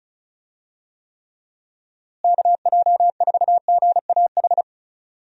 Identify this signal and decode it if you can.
Morse code, heard as a keyed tone, about 34 wpm, 700 Hz. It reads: KJ4GAH
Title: USA Call Signs - 6 Characters - Repeated 3x - 35wpm